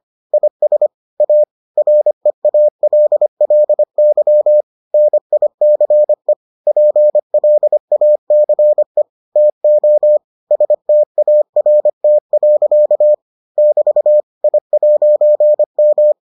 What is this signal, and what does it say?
Morse code, heard as a keyed tone, about 25 wpm, 600 Hz. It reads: IS A REALLY NICE PLACE TO START. <BT> I'M